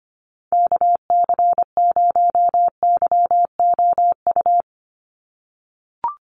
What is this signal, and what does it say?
Morse code, heard as a keyed tone, about 25 wpm, 700 Hz. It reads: KC0YOU
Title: USA Call Signs 25wpm